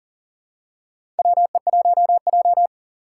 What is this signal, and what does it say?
Morse code, heard as a keyed tone, about 38 wpm, 700 Hz. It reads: WE1J
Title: USA Call Signs - 4 Characters - 6x Character Spacing - 40wpm